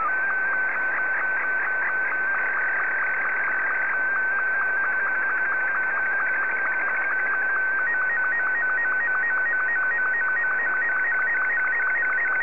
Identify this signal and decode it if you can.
SSTV (slow-scan television), Martin 2.
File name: sstv / M2-VK3MCG